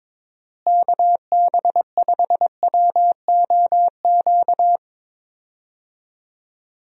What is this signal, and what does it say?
Morse code, heard as a keyed tone, about 22 wpm, 700 Hz. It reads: KB5WOQ
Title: USA Call Signs - 6 Characters - Repeated 3x - 22wpm